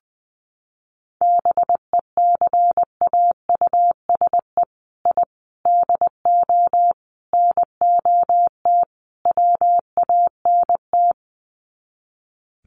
Morse code, heard as a keyed tone, about 19 wpm, 700 Hz. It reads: BECAUSE I DO NOT WANT
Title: Sets of 5 Words Encore 20wpm